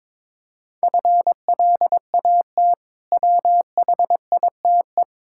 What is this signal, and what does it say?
Morse code, heard as a keyed tone, about 22 wpm, 700 Hz. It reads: FLAT WHITE
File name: Difficult Words with Hints - 22wpm